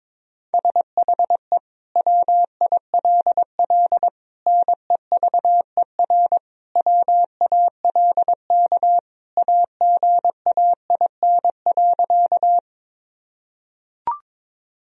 Morse code, heard as a keyed tone, about 22 wpm, 700 Hz. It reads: SHE WILL NEVER WALK AGAIN.
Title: Sentences from Top 1000 Words - Review Part 1 - 22wpm